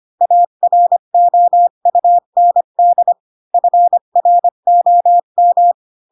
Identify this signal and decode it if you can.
Morse code, heard as a keyed tone, about 25 wpm, 700 Hz. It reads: AROUND FROM
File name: Pinocchio / Pinocchio ch4 20 25 ik1hge